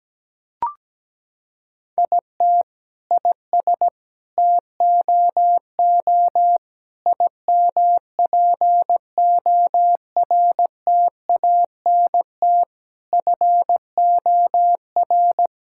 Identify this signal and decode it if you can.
Morse code, heard as a keyed tone, about 17 wpm, 700 Hz. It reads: IT IS TOO IMPORTANT FOR
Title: Sentences from Top 1000 Words - Review Part 6 - 17wpm